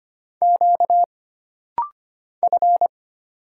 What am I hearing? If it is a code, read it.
Morse code, heard as a keyed tone, about 25 wpm, 700 Hz. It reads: Q F
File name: Single Character (Letter:Number:Punctuation:Prosign) - Rapid-Fire - 25wpm